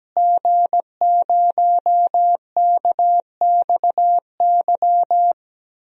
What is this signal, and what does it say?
Morse code, heard as a keyed tone, about 17 wpm, 700 Hz. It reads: G0KXY
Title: DXCC-100 Call Signs - 5 Characters - Repeated 2x - 17wpm